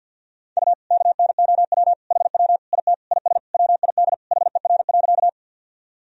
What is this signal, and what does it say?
Morse code, heard as a keyed tone, about 51 wpm, 700 Hz. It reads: U KNOW HW IT IS WID HER.
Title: Sentences from Top 100 Words and Abbreviations 50wpm